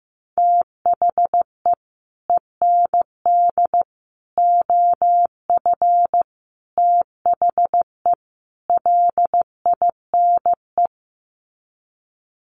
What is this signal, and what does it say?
Morse code, heard as a keyed tone, about 14 wpm, 700 Hz. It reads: THE END OF THE LINE
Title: Sets of 5 Words Encore2 15wpm